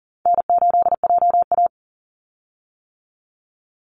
Morse code, heard as a keyed tone, about 40 wpm, 700 Hz. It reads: N8JA
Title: Easy Call Signs - 40wpm